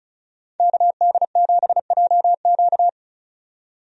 Morse code, heard as a keyed tone, about 34 wpm, 700 Hz. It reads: KD7JQ
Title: USA Call Signs - 5 Characters - 35wpm